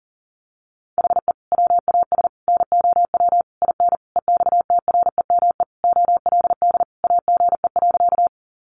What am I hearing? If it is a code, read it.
Morse code, heard as a keyed tone, about 40 wpm, 700 Hz. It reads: HE WAS NOW IN EXTREME OLD AGE.
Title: Sentences from Top 1300 Words 40wpm